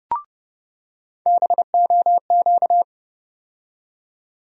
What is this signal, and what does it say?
Morse code, heard as a keyed tone, about 30 wpm, 700 Hz. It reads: BOQ
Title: USA Call Sign Postfixes - 3 Letters 30wpm